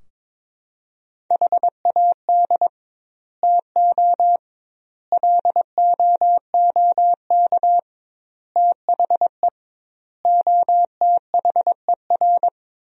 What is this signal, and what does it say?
Morse code, heard as a keyed tone, about 22 wpm, 700 Hz. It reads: HAD TO LOOK THE OTHER
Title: 2x Word-Spacing - Sentences from Top 100 Words - 22wpm